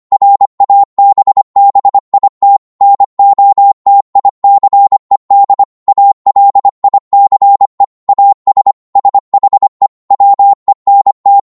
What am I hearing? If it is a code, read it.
Morse code, heard as a keyed tone, about 25 wpm, 800 Hz. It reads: RABBIT NOTICED ALICE AS SHE WENT
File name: 25wpm